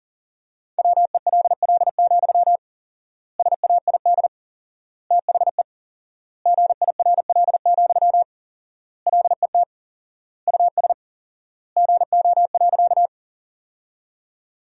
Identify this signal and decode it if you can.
Morse code, heard as a keyed tone, about 40 wpm, 700 Hz. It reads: WELL, SAID THE GIRL, LET US GO.
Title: 4x Word-Spacing - Sentences from Top 400 Words - 40wpm